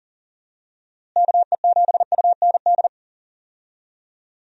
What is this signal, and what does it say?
Morse code, heard as a keyed tone, about 40 wpm, 700 Hz. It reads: KE7UND
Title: USA Call Signs - 6 Characters - Repeated 3x - 40wpm